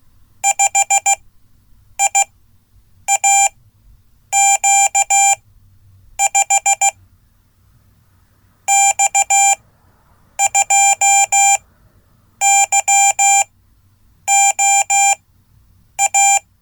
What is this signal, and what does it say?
Morse code, heard as a keyed tone, about 16 wpm, spaced as for 8 wpm, 790 Hz. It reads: 5IAQ5 X2YOA
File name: Code / 5min-a